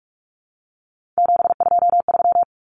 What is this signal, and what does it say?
Morse code, heard as a keyed tone, about 44 wpm, 700 Hz. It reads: 723
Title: Numbers - 3-Digit 45wpm